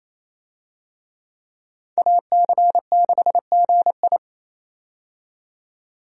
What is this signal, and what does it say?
Morse code, heard as a keyed tone, about 28 wpm, 700 Hz. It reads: AC6GI
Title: USA Call Signs - 5 Characters - 28wpm